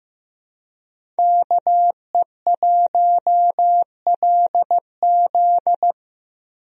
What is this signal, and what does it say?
Morse code, heard as a keyed tone, about 15 wpm, 700 Hz. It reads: KE1LZ
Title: USA Call Signs - 5 Characters - Repeated 2x - 15wpm